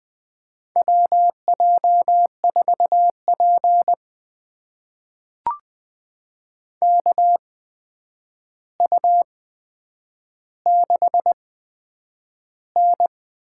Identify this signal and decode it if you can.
Morse code, heard as a keyed tone, about 20 wpm, spaced as for 9 wpm, 700 Hz. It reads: WJ4P KU6N
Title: USA Call Signs - 4 Characters - 8x Character Spacing - 20wpm